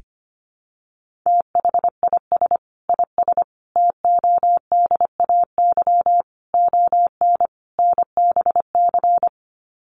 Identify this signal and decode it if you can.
Morse code, heard as a keyed tone, about 25 wpm, 700 Hz. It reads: THIS IS TODAY ON NBC
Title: Sets of 5 Words Encore 25wpm